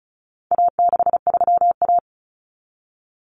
Morse code, heard as a keyed tone, about 34 wpm, 700 Hz. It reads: A63A
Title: Common Cut Numbers - 4-Digit 35wpm